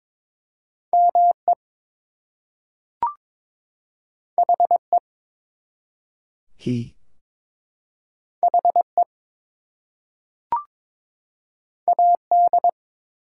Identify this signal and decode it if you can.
Morse code, heard as a keyed tone, about 22 wpm, 700 Hz. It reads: ME HE HE AD
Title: Two Letter Words - 22wpm